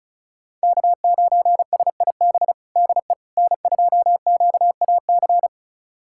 Morse code, heard as a keyed tone, about 36 wpm, 700 Hz. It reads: K9SIB DE N2QAC
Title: USA Callsign DE Callsign - 5 Characters - 35wpm